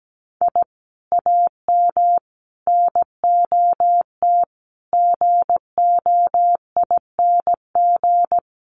Morse code, heard as a keyed tone, about 17 wpm, 700 Hz. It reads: I AM NOT GOING
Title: Sets of 5 Words Encore 17wpm